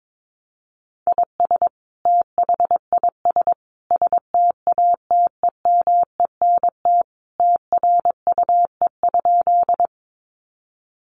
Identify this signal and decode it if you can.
Morse code, heard as a keyed tone, about 22 wpm, 700 Hz. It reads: IS THIS STATEMENT TRUE?
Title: Sentences from Top 1200 Words 22wpm